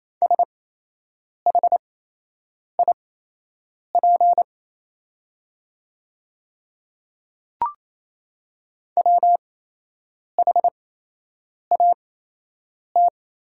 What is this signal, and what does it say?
Morse code, heard as a keyed tone, about 28 wpm, spaced as for 8 wpm, 700 Hz. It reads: SHIP WHAT
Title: 8x Character Spacing - Top 500 Words - 28wpm